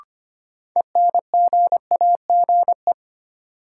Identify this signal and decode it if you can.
Morse code, heard as a keyed tone, about 25 wpm, 700 Hz. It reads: ENGAGE
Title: Sentences from Top 1500 Words - New Word Drill Alt - 25wpm